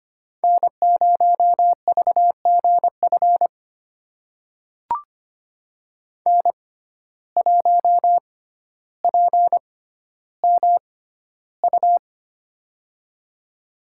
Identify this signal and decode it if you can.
Morse code, heard as a keyed tone, about 25 wpm, spaced as for 13 wpm, 700 Hz. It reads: N0VGF N1PMU
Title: USA Call Signs - 5 Characters - 6x Character Spacing - 25wpm